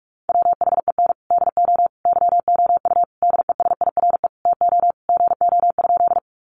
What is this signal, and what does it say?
Morse code, heard as a keyed tone, about 44 wpm, 700 Hz. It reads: WHEN DO YOU DESIRE TO GO?
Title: Sentences from Top 1100 Words 45wpm